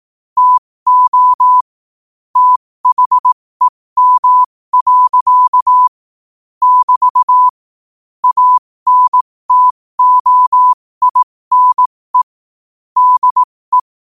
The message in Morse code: TO THEM. <BT> ANTOINE DE

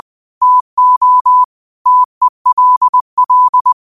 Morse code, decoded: TO TELL

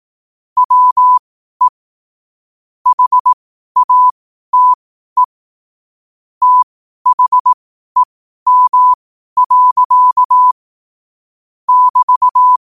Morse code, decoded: WE HATE THEM. <BT>

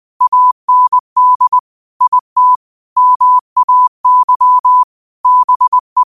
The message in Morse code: AND IT MAY BE

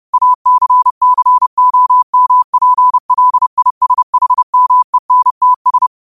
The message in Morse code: ACCOMPLISHMENTS

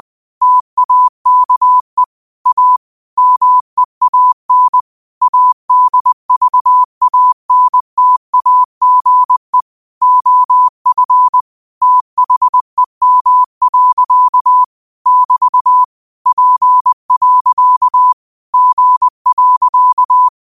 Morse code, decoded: TAKE A MEAN ADVANTAGE OF THEM. <BT> P. G.

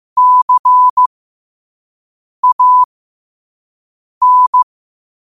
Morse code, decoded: CAN

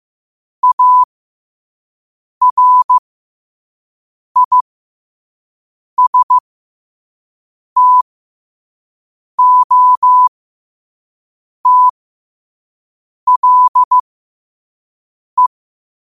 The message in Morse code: ARISTOTLE

